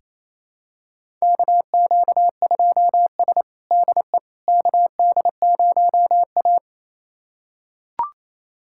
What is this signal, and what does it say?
Morse code, heard as a keyed tone, about 28 wpm, 700 Hz. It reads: KQ2S DE KD0A
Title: USA Callsign DE Callsign - 4 Characters - 28wpm